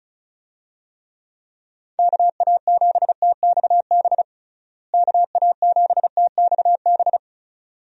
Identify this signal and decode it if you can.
Morse code, heard as a keyed tone, about 36 wpm, 700 Hz. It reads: KA7TXB KA7TXB
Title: USA Call Signs - 6 Characters - Repeated 2x - 35wpm